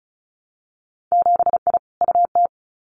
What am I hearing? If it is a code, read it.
Morse code, heard as a keyed tone, about 36 wpm, 700 Hz. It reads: 7I UT
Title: Winter Field Day Exchanges Encore 35wpm